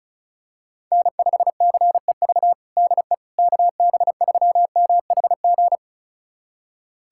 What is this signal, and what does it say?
Morse code, heard as a keyed tone, about 36 wpm, 700 Hz. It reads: N5CEV DE KB3MHG
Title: USA Callsign DE Callsign 35wpm